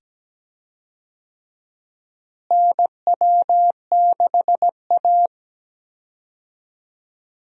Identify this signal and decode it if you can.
Morse code, heard as a keyed tone, about 17 wpm, 700 Hz. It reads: NW6A
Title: USA Call Signs - 4 Characters - Repeated 2x - 17wpm